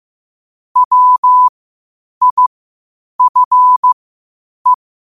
Morse code, heard as a keyed tone, about 15 wpm, spaced as for 8 wpm, 1000 Hz. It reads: WIFE
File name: qotdfiles / 8wpm